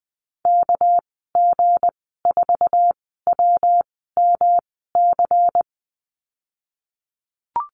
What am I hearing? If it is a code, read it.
Morse code, heard as a keyed tone, about 20 wpm, spaced as for 15 wpm, 700 Hz. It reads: KG4WMC
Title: 219 Lesson 35g - Number 0 - Call Signs 20wpm